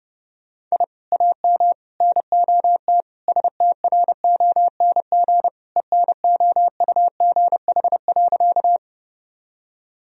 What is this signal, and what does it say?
Morse code, heard as a keyed tone, about 32 wpm, 700 Hz. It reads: I AM NOT STRONG ENOUGH.
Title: Sentences from Top 1000 Words - Review Part 3 - 30wpm